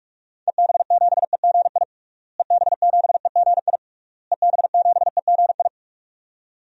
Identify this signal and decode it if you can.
Morse code, heard as a keyed tone, about 46 wpm, 700 Hz. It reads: EB7EGI EB7EGI EB7EGI
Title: DXCC-100 Call Signs - 6 Characters - Repeated 3x - 45wpm